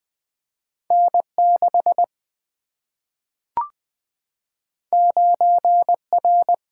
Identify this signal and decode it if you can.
Morse code, heard as a keyed tone, about 20 wpm, 700 Hz. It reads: N 6 9 R
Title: Single-Letter Single-Number 20wpm